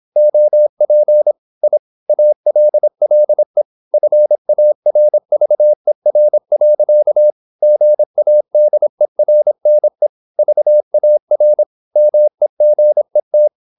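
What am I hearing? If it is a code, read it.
Morse code, heard as a keyed tone, about 26 wpm, 600 Hz. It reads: OP I ALLE FARVER. GADERNE VAR MEGET
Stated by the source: High-speed (QRQ) Morse at 26 wpm.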